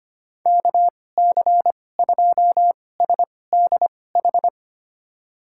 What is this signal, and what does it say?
Morse code, heard as a keyed tone, about 25 wpm, spaced as for 18 wpm, 700 Hz. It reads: KC2SDH
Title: USA Call Signs - 2x Character Spacing - 25wpm